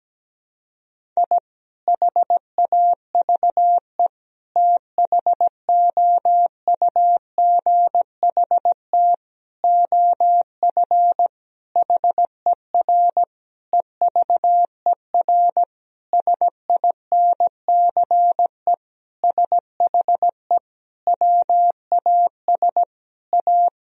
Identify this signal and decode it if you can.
Morse code, heard as a keyed tone, about 17 wpm, 700 Hz. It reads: I HAVE THOUGHT OF HER EVER SINCE SHE WAS A
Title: Sentences from Top 1000 Words - Review Part 1 - 17wpm